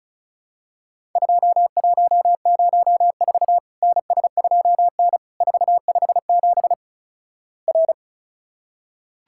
Morse code, heard as a keyed tone, about 36 wpm, 700 Hz. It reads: 2104 NS2N 457 R
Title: Worked All Europe DX Contest - QTC Exchange - 35wpm